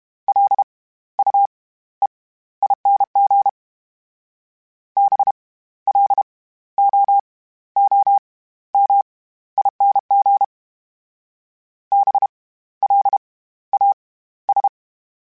Morse code, heard as a keyed tone, about 32 wpm, 800 Hz. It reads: L U E ING B L O O M ING B L A S
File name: Special INGs / Special INGs-mp3s / SlowToFaster ing @30